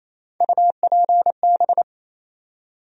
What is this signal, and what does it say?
Morse code, heard as a keyed tone, about 28 wpm, 700 Hz. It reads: UPB
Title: USA Call Sign Postfixes - 3 Letters 28wpm